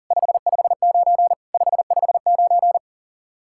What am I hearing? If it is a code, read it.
Morse code, heard as a keyed tone, about 50 wpm, 700 Hz. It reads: 559 559
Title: Speed-Racing Common QSO Phrases Encore - 40wpm